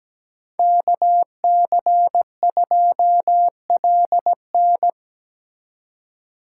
Morse code, heard as a keyed tone, about 20 wpm, 700 Hz. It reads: KC2LN
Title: Speed Racing - USA 2x2 Call Signs 17wpm